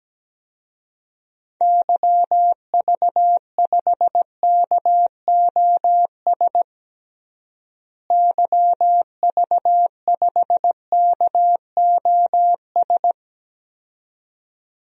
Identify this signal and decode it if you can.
Morse code, heard as a keyed tone, about 17 wpm, 700 Hz. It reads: YV5KOS YV5KOS
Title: DXCC-100 Call Signs - 6 Characters - Repeated 2x - 17wpm